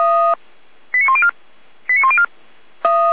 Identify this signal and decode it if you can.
Five-tone selective call (ZVEI), heard as quick runs of five tones, tones of about 70 ms. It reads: 89104 89104